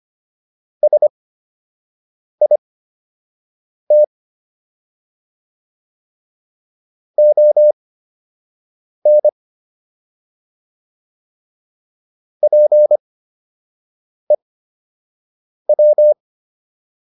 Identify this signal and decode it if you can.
Morse code, heard as a keyed tone, about 25 wpm, spaced as for 6 wpm, 600 Hz. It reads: SIT ON PEW